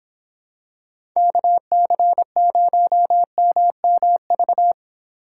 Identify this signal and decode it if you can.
Morse code, heard as a keyed tone, about 26 wpm, 700 Hz. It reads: KC0MMV